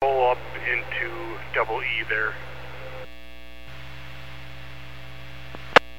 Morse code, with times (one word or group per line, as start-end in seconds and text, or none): none